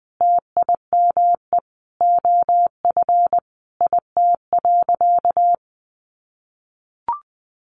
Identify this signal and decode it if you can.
Morse code, heard as a keyed tone, about 20 wpm, 700 Hz. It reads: TIME OF IT.
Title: Sentences from Top 1200 Words 20wpm